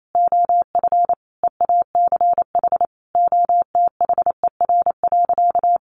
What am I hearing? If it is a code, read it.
Morse code, heard as a keyed tone, about 28 wpm, 700 Hz. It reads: OF EACH OTHER.
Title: Sentences from Top 700 Words 28wpm